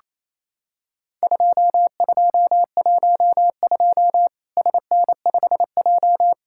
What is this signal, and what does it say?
Morse code, heard as a keyed tone, about 28 wpm, 700 Hz. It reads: 2212 SN5J
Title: Worked All Europe DX Contest - QTC Exchange - 28wpm